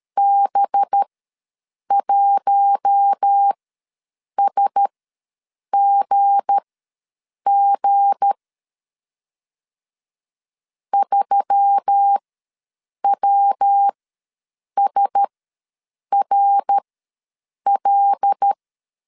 Morse code, heard as a keyed tone, about 13 wpm, spaced as for 7 wpm, 800 Hz. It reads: B1SGG 3WSRL
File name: LEZIONI / 7 GF9Z / 7 G ws 12wpm 3140